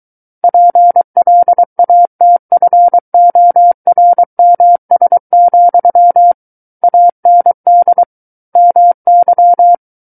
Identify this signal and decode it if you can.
Morse code, heard as a keyed tone, about 23 wpm, 700 Hz. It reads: PLATFORMS, AND MY
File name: Book One / 14-Chapter 14 - 23 WPM